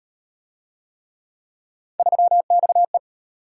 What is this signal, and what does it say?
Morse code, heard as a keyed tone, about 37 wpm, 700 Hz. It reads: 3XE